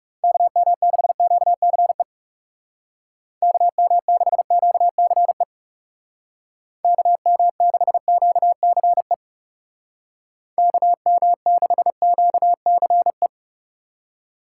Morse code, heard as a keyed tone, about 39 wpm, 700 Hz. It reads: KM6QCE KM6QCE KM6QCE KM6QCE